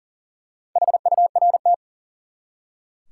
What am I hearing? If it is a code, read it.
Morse code, heard as a keyed tone, about 40 wpm, 700 Hz. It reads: HURT
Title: Four Letter Words - 40wpm